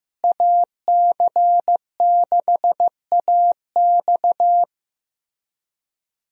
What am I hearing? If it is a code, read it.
Morse code, heard as a keyed tone, about 15 wpm, 700 Hz. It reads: AC6AX
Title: USA Call Signs - 5 Characters - Repeated 3x - 15wpm